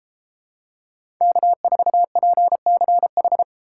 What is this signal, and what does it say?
Morse code, heard as a keyed tone, about 32 wpm, 700 Hz. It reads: K4PCH